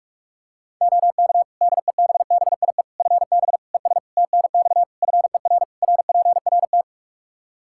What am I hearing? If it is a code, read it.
Morse code, heard as a keyed tone, about 46 wpm, 700 Hz. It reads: OK DEBBIE FB ES TNX FER RPRT
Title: Practice QSO - Exchange 2 - Info - 45wpm